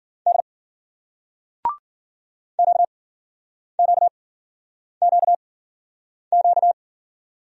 Morse code, heard as a keyed tone, about 53 wpm, spaced as for 9 wpm, 700 Hz. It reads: D QQQQ